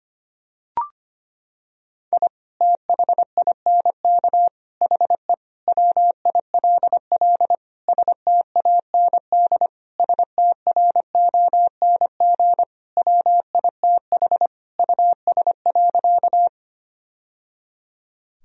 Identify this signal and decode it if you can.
Morse code, heard as a keyed tone, about 25 wpm, 700 Hz. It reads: I THINK HE WILL STAND STRONG WITH US.
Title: Sentences from Top 1000 Words - Review Part 3 - 25wpm